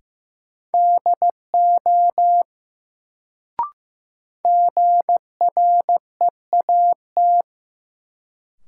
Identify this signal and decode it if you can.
Morse code, heard as a keyed tone, about 15 wpm, 700 Hz. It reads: DO GREAT
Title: Sets of 2 Words - New Word Drill - 15wpm